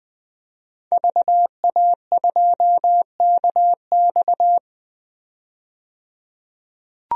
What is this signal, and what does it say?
Morse code, heard as a keyed tone, about 20 wpm, 700 Hz. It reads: VA2KX
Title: DXCC-100 Call Signs - 5 Characters - 20wpm